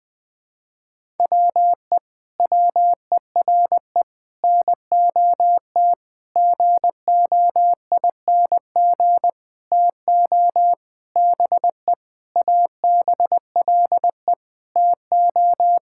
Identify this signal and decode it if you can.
Morse code, heard as a keyed tone, about 20 wpm, 700 Hz. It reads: WE WERE NOT GOING TO BE ABLE TO
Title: Sentences with 7-Word Phrases - Spoken Beginning - 20wpm